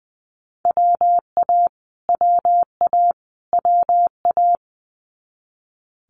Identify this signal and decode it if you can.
Morse code, heard as a keyed tone, about 19 wpm, 700 Hz. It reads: WA WA WA
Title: Multi-letter Phonemes Repeated 3x 20wpm